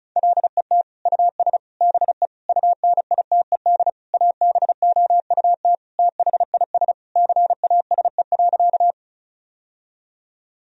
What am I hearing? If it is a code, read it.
Morse code, heard as a keyed tone, about 36 wpm, 700 Hz. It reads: LET US BE UNITED ABOUT THIS CASE.
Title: Sentences from Top 1000 Words - Review Part 4 - 35wpm